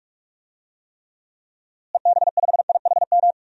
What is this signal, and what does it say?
Morse code, heard as a keyed tone, about 47 wpm, 700 Hz. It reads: EB5IHM
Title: DXCC-100 Call Signs - 6 Characters - Repeated 3x - 45wpm